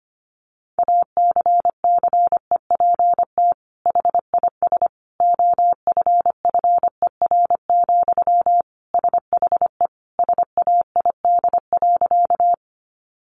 Morse code, heard as a keyed tone, about 25 wpm, 700 Hz. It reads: ACCEPT HIS OFFER, SHE SAID.
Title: Sentences from Top 1300 Words 25wpm